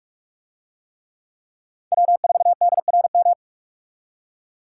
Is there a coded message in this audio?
Yes: Morse code from 1.9 to 3.3 s, heard as a keyed tone, about 45 wpm, 700 Hz. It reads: W4DRM